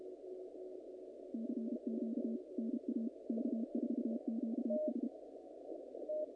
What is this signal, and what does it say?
Morse code, heard as a keyed tone, about 31 wpm, 220 Hz. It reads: CQ NA K4QS
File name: contests / naqp-cw-k5tr-op-k5pi- -2100